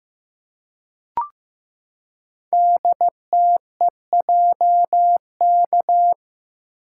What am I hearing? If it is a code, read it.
Morse code, heard as a keyed tone, about 15 wpm, 700 Hz. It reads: DTEJK
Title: Code Groups - 5 Letters - 15wpm